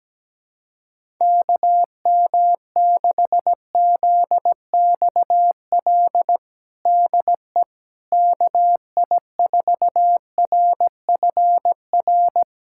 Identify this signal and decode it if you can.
Morse code, heard as a keyed tone, about 17 wpm, 700 Hz. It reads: KM6ZXL DE KI4RFR
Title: USA Callsign DE Callsign 17wpm